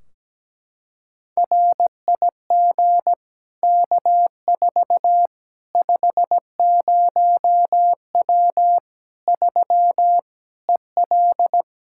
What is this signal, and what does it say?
Morse code, heard as a keyed tone, about 17 wpm, 700 Hz. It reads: RIG K4 50W 3 EL
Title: Practice QSO - Exchange 2 - Info - 17wpm